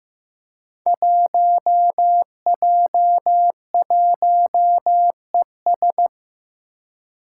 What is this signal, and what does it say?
Morse code, heard as a keyed tone, about 15 wpm, 700 Hz. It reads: 1J1ES
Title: Five Challenging Letter-Number Combinations 15wpm